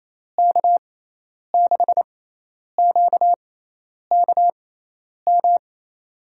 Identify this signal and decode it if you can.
Morse code, heard as a keyed tone, about 28 wpm, spaced as for 10 wpm, 700 Hz. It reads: K6QKM
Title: USA Call Signs - 6 Characters - 6x Character Spacing - 28wpm